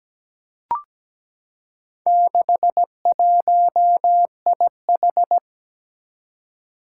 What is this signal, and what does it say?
Morse code, heard as a keyed tone, about 17 wpm, 700 Hz. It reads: 61IH
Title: Four Challenging Letter-Number Combinations 17wpm